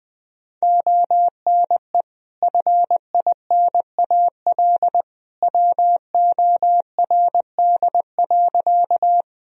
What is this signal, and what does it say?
Morse code, heard as a keyed tone, about 20 wpm, 700 Hz. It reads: ONE FINAL WORD.
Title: Sentences from Top 1000 Words - Review Part 2 - 20wpm